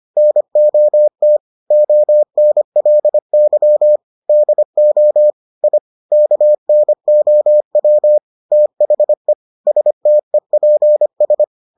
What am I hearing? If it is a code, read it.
Morse code, heard as a keyed tone, about 25 wpm, 600 Hz. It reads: NOT ONLY DO I KNOW THE STEPS